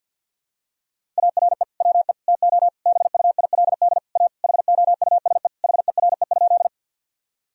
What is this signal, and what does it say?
Morse code, heard as a keyed tone, about 51 wpm, 700 Hz. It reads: ARE WE TO BUILD A HOUSE HERE?
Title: Sentences from Top 1000 Words - Review Part 1 - 50wpm